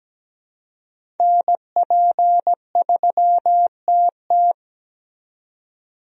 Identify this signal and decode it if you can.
Morse code, heard as a keyed tone, about 17 wpm, 700 Hz. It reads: NP3TT
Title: USA Call Signs - 5 Characters - 17wpm